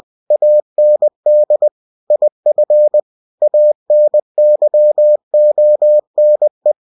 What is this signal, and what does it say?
Morse code, heard as a keyed tone, about 20 wpm, 600 Hz. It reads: AND IF ANYONE
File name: cw-books / winnie-the-pooh / winnie-20wpm-page-0028